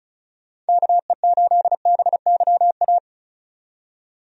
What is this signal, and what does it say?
Morse code, heard as a keyed tone, about 36 wpm, 700 Hz. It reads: KE8BYA
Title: USA Call Signs - 6 Characters - 35wpm